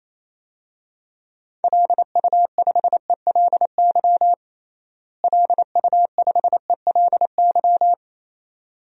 Morse code, heard as a keyed tone, about 28 wpm, 700 Hz. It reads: LU5ELY LU5ELY
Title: DXCC-100 Call Signs - 6 Characters - Repeated 2x - 28wpm